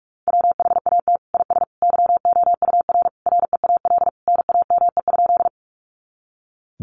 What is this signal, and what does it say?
Morse code, heard as a keyed tone, about 44 wpm, 700 Hz. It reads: WHAT IS YOUR REAL NAME?
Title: Sentences from Top 300 Words 45wpm